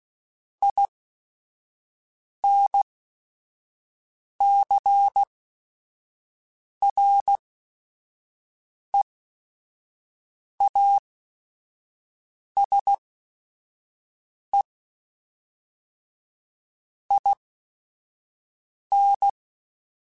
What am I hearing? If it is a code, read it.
Morse code, heard as a keyed tone, about 16 wpm, spaced as for 5 wpm, 800 Hz. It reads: INCREASE IN
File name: Technical Subjects / Technical Subjects-mp3s / XL @5